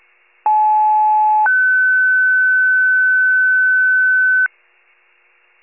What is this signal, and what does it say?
Two-tone page, heard as a one-second tone, then a three-second tone, about 860 then 1500 Hz. It is heard from 0.5 to 4.5 s.
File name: pager / 94 RESCUE / wcalert